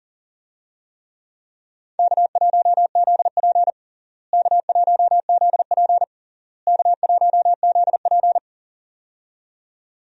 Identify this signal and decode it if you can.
Morse code, heard as a keyed tone, about 40 wpm, 700 Hz. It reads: K1ZP K1ZP K1ZP
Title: USA Call Signs - 4 Characters - Repeated 3x - 40wpm